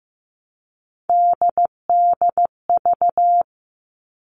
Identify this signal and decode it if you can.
Morse code, heard as a keyed tone, about 15 wpm, 700 Hz. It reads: DDV
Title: All Cut Numbers - 3-Digit 15wpm